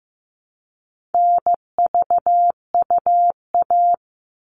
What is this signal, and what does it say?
Morse code, heard as a keyed tone, about 15 wpm, 700 Hz. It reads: NVUA
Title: All Cut Numbers - 4-Digit 15wpm